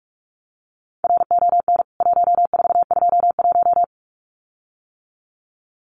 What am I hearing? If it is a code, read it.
Morse code, heard as a keyed tone, about 44 wpm, 700 Hz. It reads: RON 1421
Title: CWT Exchange 45wpm